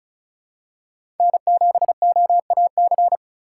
Morse code, heard as a keyed tone, about 34 wpm, 700 Hz. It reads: N7OAC
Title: USA Call Signs - 5 Characters - 4x Character Spacing - 35wpm